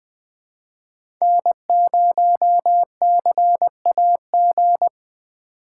Morse code, heard as a keyed tone, about 20 wpm, 700 Hz. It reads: N0CAG
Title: USA Call Signs - 5 Characters - 3x Character Spacing - 20wpm